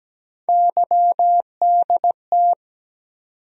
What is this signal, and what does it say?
Morse code, heard as a keyed tone, about 17 wpm, 700 Hz. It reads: YDT
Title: USA Call Sign Postfixes - 3 Letters 17wpm